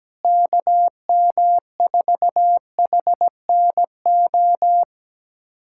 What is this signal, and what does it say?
Morse code, heard as a keyed tone, about 17 wpm, 700 Hz. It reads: KM4HNO
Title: USA Callsign DE Callsign - 6 Characters - 17wpm